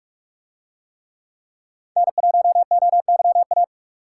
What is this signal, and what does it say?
Morse code, heard as a keyed tone, about 41 wpm, 700 Hz. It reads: N1OYA